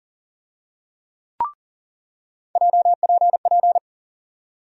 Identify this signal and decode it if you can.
Morse code, heard as a keyed tone, about 40 wpm, 700 Hz. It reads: JPP
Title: USA Call Sign Postfixes - 3 Letters 40wpm